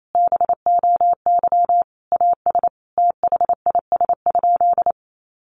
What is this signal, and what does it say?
Morse code, heard as a keyed tone, about 28 wpm, 700 Hz. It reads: BOY AS THIS?
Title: Sentences from Top 400 Words 28wpm